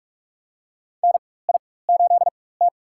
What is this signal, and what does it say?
Morse code, heard as a keyed tone, about 42 wpm, spaced as for 21 wpm, 700 Hz. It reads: NI8T
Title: USA Call Signs - 4 Characters - 4x Character Spacing - 45wpm